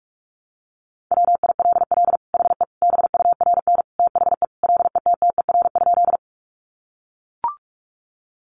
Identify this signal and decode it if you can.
Morse code, heard as a keyed tone, about 44 wpm, 700 Hz. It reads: WILL HE BURN THE LETTER?
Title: Sentences from Top 1300 Words 45wpm